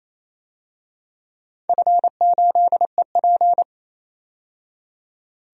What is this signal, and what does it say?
Morse code, heard as a keyed tone, about 29 wpm, 700 Hz. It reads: F8EP
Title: DXCC-100 Call Signs - 4 Characters - 28wpm